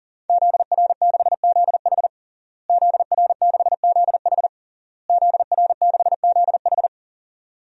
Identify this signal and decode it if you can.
Morse code, heard as a keyed tone, about 40 wpm, 700 Hz. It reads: ZR6ZH ZR6ZH ZR6ZH
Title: DXCC-100 Call Signs - 5 Characters - Repeated 3x - 40wpm